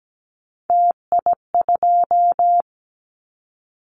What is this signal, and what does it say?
Morse code, heard as a keyed tone, about 17 wpm, 700 Hz. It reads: TI2
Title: Callsign Prefixes for Top 100 DXCC - 17wpm